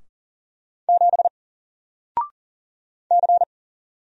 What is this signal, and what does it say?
Morse code, heard as a keyed tone, about 40 wpm, 700 Hz. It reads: 7 C
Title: Single Letter-Number 40wpm